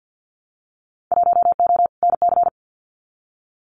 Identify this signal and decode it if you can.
Morse code, heard as a keyed tone, about 50 wpm, 700 Hz. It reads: 1O NC
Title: Winter Field Day Exchanges 50wpm